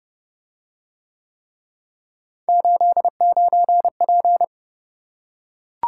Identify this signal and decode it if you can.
Morse code, heard as a keyed tone, about 28 wpm, 700 Hz. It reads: 89P